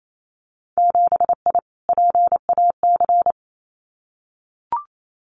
Morse code, heard as a keyed tone, about 28 wpm, 700 Hz. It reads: 7I PAC
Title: Winter Field Day Exchanges Encore 28wpm